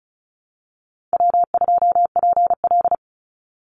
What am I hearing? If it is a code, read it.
Morse code, heard as a keyed tone, about 34 wpm, 700 Hz. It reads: W2PL
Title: Call Signs - 35wpm